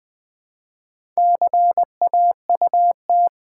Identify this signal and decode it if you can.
Morse code, heard as a keyed tone, about 20 wpm, 700 Hz. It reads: CAUT
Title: Build A Word - 2 Letters At A Time - 8-Letter Words - 20wpm